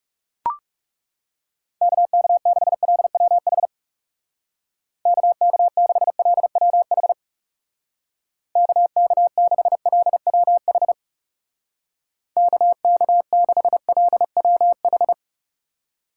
Morse code, heard as a keyed tone, about 37 wpm, 700 Hz. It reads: KK6LWH KK6LWH KK6LWH KK6LWH
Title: Speed Racing - USA 2x3 Call Signs 30wpm